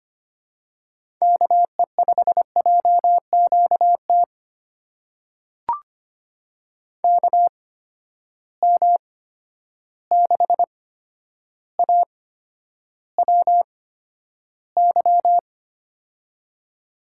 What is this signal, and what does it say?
Morse code, heard as a keyed tone, about 25 wpm, spaced as for 11 wpm, 700 Hz. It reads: KE5JQT KM6AWY
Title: USA Call Signs - 6 Characters - 8x Character Spacing - 25wpm